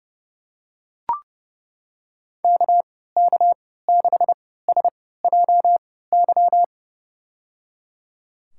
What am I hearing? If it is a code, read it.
Morse code, heard as a keyed tone, about 31 wpm, spaced as for 17 wpm, 700 Hz. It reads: KK6SJY
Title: USA Call Signs - 6 Characters - 3x Character Spacing - 30wpm